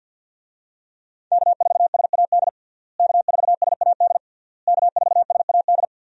Morse code, heard as a keyed tone, about 50 wpm, 700 Hz. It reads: K4SAD K4SAD K4SAD
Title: USA Call Signs - 5 Characters - Repeated 3x - 50wpm